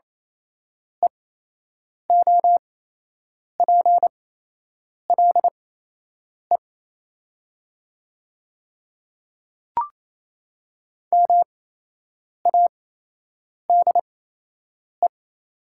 Morse code, heard as a keyed tone, about 28 wpm, spaced as for 8 wpm, 700 Hz. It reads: EOPLE MADE
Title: 8x Character Spacing - Top 200 Words - 28wpm